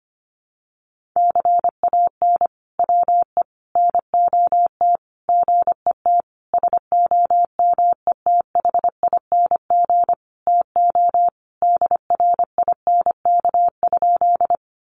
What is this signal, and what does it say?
Morse code, heard as a keyed tone, about 25 wpm, 700 Hz. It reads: CAN WE NOT GET SOMETHING TO DRINK?
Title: Sentences from Top 1500 Words 25wpm